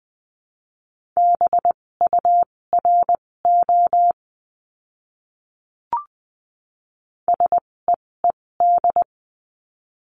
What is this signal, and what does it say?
Morse code, heard as a keyed tone, about 20 wpm, spaced as for 16 wpm, 700 Hz. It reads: BURO SEED
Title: 256 Lesson 42a - Words 12-20wpm